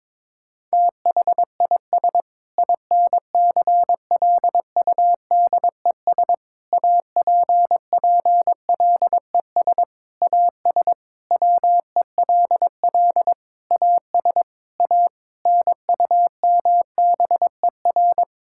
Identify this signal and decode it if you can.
Morse code, heard as a keyed tone, about 22 wpm, 700 Hz. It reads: THIS INCLUDES APPLES AS WELL AS A NUMBER
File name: Sentences with 7-Word Phrases - 22wpm